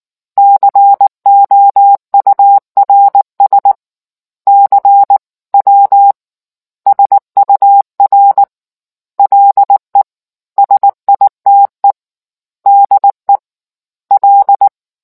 Morse code, heard as a keyed tone, about 19 wpm, 800 Hz. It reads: COURS CW SUR LE SITE DE L